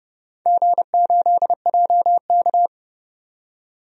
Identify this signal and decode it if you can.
Morse code, heard as a keyed tone, about 31 wpm, 700 Hz. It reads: G8JK
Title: DXCC-100 Call Signs - 4 Characters - 30wpm